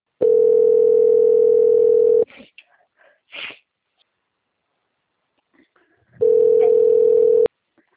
Ringback tone (North America), 440 + 480 Hz.